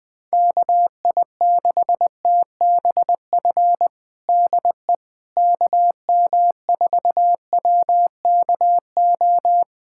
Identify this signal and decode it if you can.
Morse code, heard as a keyed tone, about 19 wpm, 700 Hz. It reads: KI6TBF DE KM4WKO
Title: USA Callsign DE Callsign 20wpm